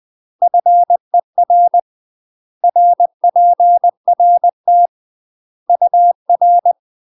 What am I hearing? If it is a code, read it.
Morse code, heard as a keyed tone, about 20 wpm, 700 Hz. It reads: FER RPRT UR
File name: UK MASTER QSOs TRAINING / 20WPM / UK MASTER QSOs 0008